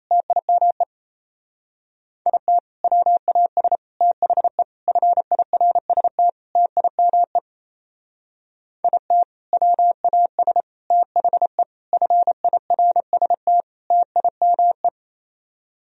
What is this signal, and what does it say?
Morse code, heard as a keyed tone, about 34 wpm, 700 Hz. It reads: TIME IT WAS THE FIRST TIME IT WAS THE FIRST TIME
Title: Speed-Racing - Sets of 5 Words 28wpm